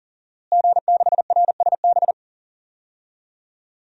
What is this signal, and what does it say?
Morse code, heard as a keyed tone, about 37 wpm, 700 Hz. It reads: G6RSB
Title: DXCC-100 Call Signs - 5 Characters - Repeated 3x - 40wpm